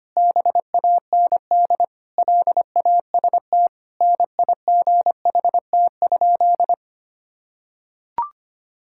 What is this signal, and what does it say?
Morse code, heard as a keyed tone, about 25 wpm, 700 Hz. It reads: BAND LAST NIGHT?
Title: Sentences from Top 1600 Words 25wpm